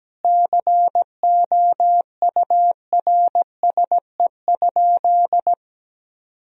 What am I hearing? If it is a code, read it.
Morse code, heard as a keyed tone, about 17 wpm, 700 Hz. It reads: COURSE?
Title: Sentences from Top 1000 Words - Review Part 2 - 17wpm